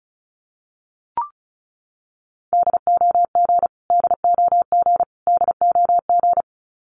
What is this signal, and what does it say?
Morse code, heard as a keyed tone, about 34 wpm, 700 Hz. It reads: DOG DOG DOG
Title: Top 500 Words Repeated 35wpm